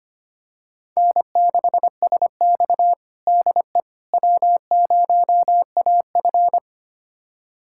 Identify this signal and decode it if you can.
Morse code, heard as a keyed tone, about 25 wpm, 700 Hz. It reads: N6SX DE W0AF
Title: USA Callsign DE Callsign - 4 Characters - 25wpm